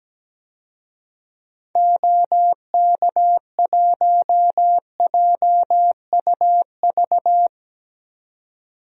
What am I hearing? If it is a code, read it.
Morse code, heard as a keyed tone, about 17 wpm, 700 Hz. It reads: OK1JUV
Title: DXCC-100 Call Signs - 6 Characters - 17wpm